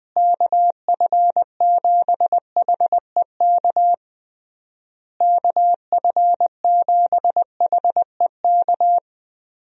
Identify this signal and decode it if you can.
Morse code, heard as a keyed tone, about 20 wpm, 700 Hz. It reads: KF7HEK KF7HEK
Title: USA Call Signs - 6 Characters - Repeated 3x - 20wpm